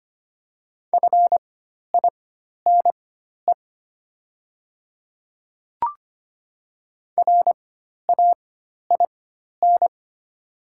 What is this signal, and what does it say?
Morse code, heard as a keyed tone, about 25 wpm, spaced as for 12 wpm, 700 Hz. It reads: FINE RAIN
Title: 4x Character Spacing - Top 500 Words - 25wpm